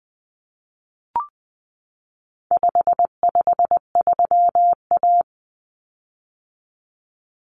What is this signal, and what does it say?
Morse code, heard as a keyed tone, about 20 wpm, 700 Hz. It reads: 553A
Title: Common Cut Numbers - 4-Digit 20wpm